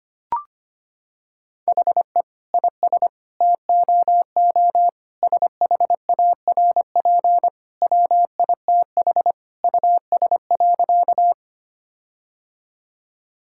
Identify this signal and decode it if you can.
Morse code, heard as a keyed tone, about 25 wpm, 700 Hz. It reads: HE IS TOO SHARP WITH US.
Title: Sentences from Top 1600 Words 25wpm